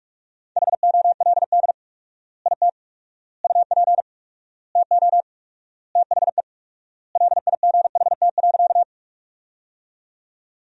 Morse code, heard as a keyed tone, about 46 wpm, 700 Hz. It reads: HOLD IT UP TO THE LIGHT.
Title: 4x Word-Spacing - Sentences from Top 400 Words - 45wpm